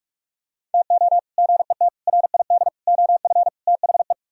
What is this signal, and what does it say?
Morse code, heard as a keyed tone, about 48 wpm, 700 Hz. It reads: TO GET RID OF THE